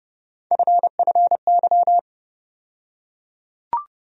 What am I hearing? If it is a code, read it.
Morse code, heard as a keyed tone, about 31 wpm, 700 Hz. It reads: FFY
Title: USA Call Sign Postfixes - 3 Letters 30wpm